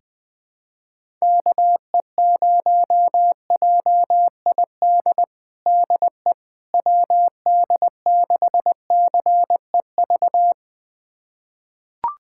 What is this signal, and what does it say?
Morse code, heard as a keyed tone, about 20 wpm, 700 Hz. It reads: KE0JID DE WD6CEV
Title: USA Callsign DE Callsign - 6 Characters - 20wpm